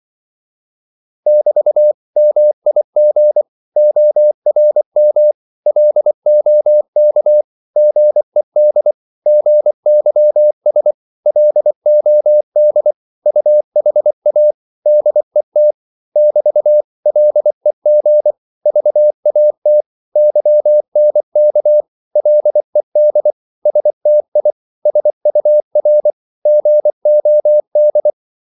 Morse code, heard as a keyed tone, about 24 wpm, 600 Hz. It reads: <BT> MIG ORM LOK GED GYS LOD UHA DET <BT> LEG VAT YNK LED STI SUR GOD
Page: Korte danske ord 24wpm